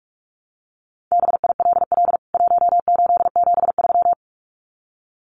Morse code, heard as a keyed tone, about 44 wpm, 700 Hz. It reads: BILL 1873
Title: CWT Exchange 45wpm